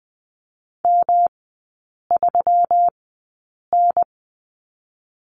Morse code, heard as a keyed tone, about 20 wpm, 700 Hz. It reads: M 3 N
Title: 121 Lesson 20e - Number 3 - Three Characters 20wpm